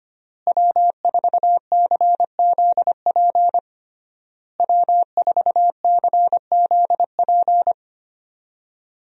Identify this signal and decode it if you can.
Morse code, heard as a keyed tone, about 25 wpm, 700 Hz. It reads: W4CZP W4CZP
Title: USA Call Signs - 5 Characters - Repeated 3x - 25wpm